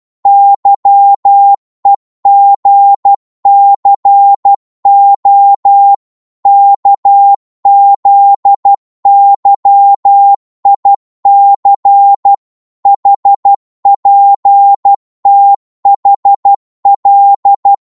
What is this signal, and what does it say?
Morse code, heard as a keyed tone, about 12 wpm, 800 Hz. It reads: YEGCO KZYIC HPTHL
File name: Morse / ltr-12wpm